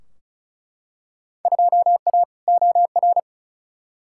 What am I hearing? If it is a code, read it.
Morse code, heard as a keyed tone, about 35 wpm, 700 Hz. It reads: 2A OR
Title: Field Day Exchanges Encore - 35wpm